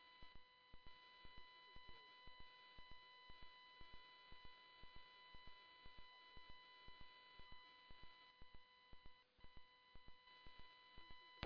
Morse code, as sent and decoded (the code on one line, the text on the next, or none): none